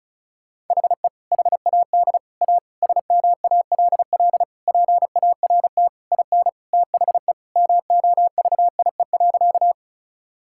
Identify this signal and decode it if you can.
Morse code, heard as a keyed tone, about 36 wpm, 700 Hz. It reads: HE HAD A SMALL PART IN THE MOVIE.
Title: Sentences from Top 1000 Words - Review Part 5 - 35wpm